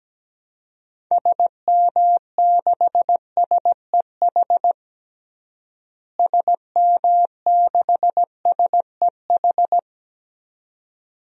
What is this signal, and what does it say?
Morse code, heard as a keyed tone, about 17 wpm, 700 Hz. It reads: SM6SEH SM6SEH
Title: DXCC-100 Call Signs - 6 Characters - Repeated 2x - 17wpm